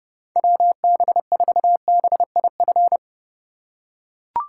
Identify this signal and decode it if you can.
Morse code, heard as a keyed tone, about 31 wpm, 700 Hz. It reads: WB4BIF
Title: USA Call Signs - 6 Characters - 4x Character Spacing - 30wpm